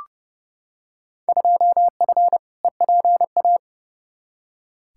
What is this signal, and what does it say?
Morse code, heard as a keyed tone, about 30 wpm, 700 Hz. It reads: 2F EPA
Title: Field Day Exchanges - 30wpm